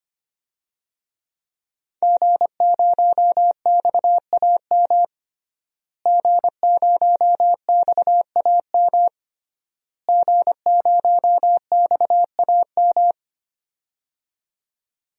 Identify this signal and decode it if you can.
Morse code, heard as a keyed tone, about 25 wpm, 700 Hz. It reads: G0XAM G0XAM G0XAM
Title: DXCC-100 Call Signs - 5 Characters - Repeated 3x - 25wpm